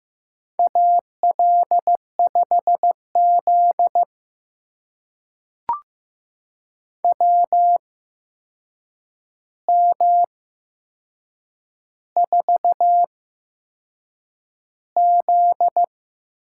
Morse code, heard as a keyed tone, about 15 wpm, spaced as for 6 wpm, 700 Hz. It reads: AL5Z WM4Z
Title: USA Call Signs - 4 Characters - 8x Character Spacing - 15wpm